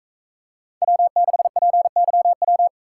Morse code, heard as a keyed tone, about 42 wpm, 700 Hz. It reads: W6PYW